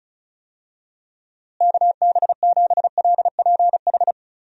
Morse code, heard as a keyed tone, about 34 wpm, 700 Hz. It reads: KB7LPH
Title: USA Call Signs - 6 Characters - Repeated 3x - 35wpm